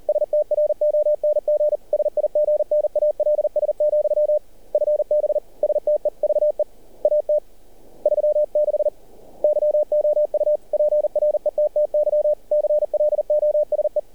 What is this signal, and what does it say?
Morse code, heard as a keyed tone, about 38 wpm, 600 Hz. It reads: STRONG SIGNALS, FB STEVE AT 36 YOU PRETTY CLOSE
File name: archived / CRSnet